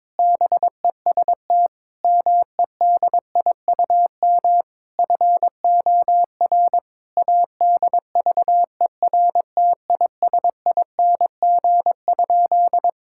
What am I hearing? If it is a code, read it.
Morse code, heard as a keyed tone, about 21 wpm, 700 Hz. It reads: BEST MEDIUM FOR ADVERTISING?
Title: Sentences from Top 1000 Words - Review Part 6 - 22wpm